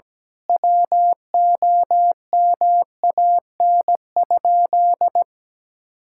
Morse code, heard as a keyed tone, about 17 wpm, 700 Hz. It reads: WOMAN?
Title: Sentences from Top 1600 Words 17wpm